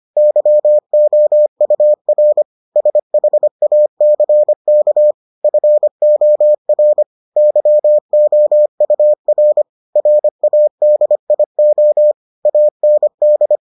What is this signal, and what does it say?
Morse code, heard as a keyed tone, about 25 wpm, 600 Hz. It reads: YOUR SHACK FOR YOUR RADIO AND